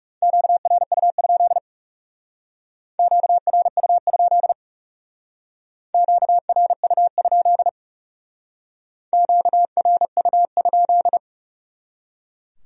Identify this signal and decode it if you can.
Morse code, heard as a keyed tone, about 39 wpm, 700 Hz. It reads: QRU? QRU? QRU? QRU?